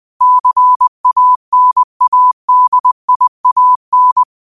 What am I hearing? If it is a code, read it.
Morse code, heard as a keyed tone, about 20 wpm, 1000 Hz. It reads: CANADIAN